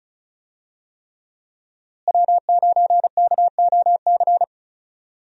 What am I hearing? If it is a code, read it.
Morse code, heard as a keyed tone, about 36 wpm, 700 Hz. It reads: W9KOC
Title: USA Call Signs - 5 Characters - Repeated 3x - 35wpm